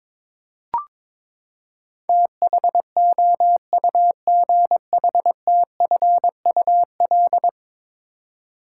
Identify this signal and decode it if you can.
Morse code, heard as a keyed tone, about 21 wpm, 700 Hz. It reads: THOUGHTFUL
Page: morse-code-ninja-practice-22wpm